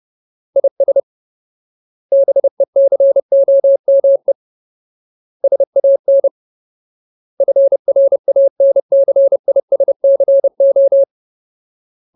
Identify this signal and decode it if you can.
Morse code, heard as a keyed tone, about 30 wpm, 550 Hz. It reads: IS BECOME SAN FRANCISCO
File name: Ultimate Word List / 30WPM Ultimate